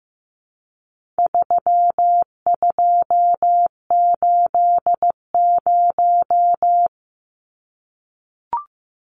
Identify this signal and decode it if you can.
Morse code, heard as a keyed tone, about 15 wpm, 700 Hz. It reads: 3280
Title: Numbers - 4-Digit 15wpm